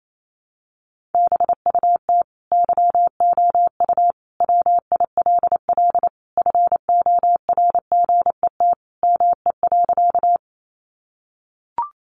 Morse code, heard as a keyed tone, about 28 wpm, 700 Hz. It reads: BUT YOU WILL FORGET ME.
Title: Sentences from Top 1200 Words 28wpm